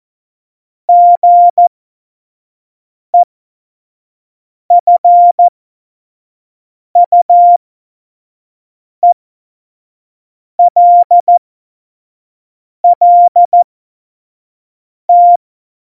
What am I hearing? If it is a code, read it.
Morse code, heard as a keyed tone, about 14 wpm, spaced as for 5 wpm, 700 Hz. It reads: GEFUELLT
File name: Morsekurs / Witze / Witz 12. September 17, Tempo 25